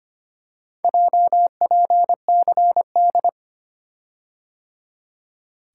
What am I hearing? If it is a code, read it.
Morse code, heard as a keyed tone, about 25 wpm, 700 Hz. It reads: JPCD
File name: practice / 25wpm / Four Letter-Number Combinations - 25wpm